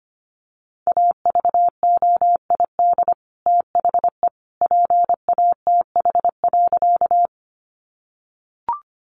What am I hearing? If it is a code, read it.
Morse code, heard as a keyed tone, about 25 wpm, 700 Hz. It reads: AVOID THE PATH.
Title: Sentences from Top 1400 Words 25wpm